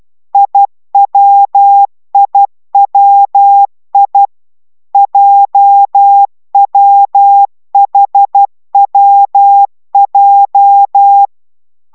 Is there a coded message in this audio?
Yes: Morse code from 0.3 to 11.3 s, heard as a keyed tone, about 12 wpm, 800 Hz. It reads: IWIWI JWHWJ